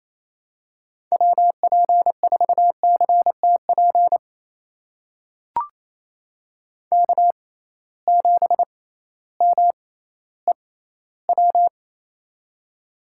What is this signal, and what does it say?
Morse code, heard as a keyed tone, about 28 wpm, spaced as for 15 wpm, 700 Hz. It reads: WP4CTP K7MEW
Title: USA Call Signs - 6x Character Spacing - 28wpm